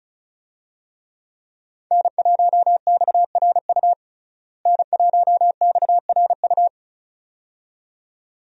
Morse code, heard as a keyed tone, about 36 wpm, 700 Hz. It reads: N1XRU N1XRU
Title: USA Call Signs - 5 Characters - Repeated 2x - 35wpm